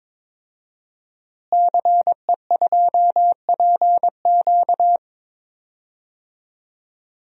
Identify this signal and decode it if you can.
Morse code, heard as a keyed tone, about 22 wpm, 700 Hz. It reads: CE2PQ
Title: DXCC-100 Call Signs - 5 Characters - Repeated 2x - 22wpm